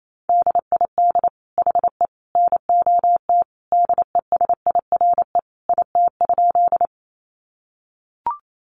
Morse code, heard as a keyed tone, about 28 wpm, 700 Hz. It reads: DID HE NOT DESIRE IT?
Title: Sentences from Top 1100 Words 28wpm